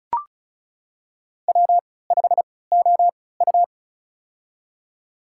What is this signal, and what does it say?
Morse code, heard as a keyed tone, about 34 wpm, spaced as for 20 wpm, 700 Hz. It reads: W5OU
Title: USA Call Signs - 4 Characters - 3x Character Spacing - 35wpm